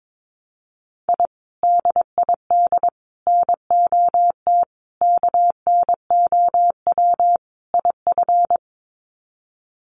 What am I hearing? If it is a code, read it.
Morse code, heard as a keyed tone, about 22 wpm, 700 Hz. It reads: I DID NOT KNOW IF
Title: Sets of 5 Words Encore 22wpm